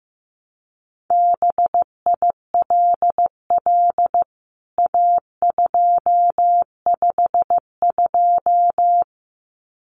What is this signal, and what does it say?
Morse code, heard as a keyed tone, about 15 wpm, 700 Hz. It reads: BILL A252
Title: CWT Exchange 15wpm